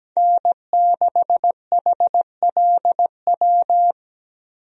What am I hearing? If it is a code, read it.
Morse code, heard as a keyed tone, about 17 wpm, 700 Hz. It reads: N6HLW
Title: USA Call Signs - 5 Characters - Repeated 3x - 17wpm